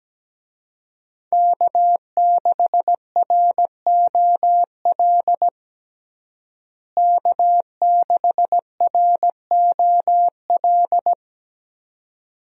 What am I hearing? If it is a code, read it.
Morse code, heard as a keyed tone, about 17 wpm, 700 Hz. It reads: K6ROL K6ROL
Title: USA Call Signs - 5 Characters - Repeated 2x - 17wpm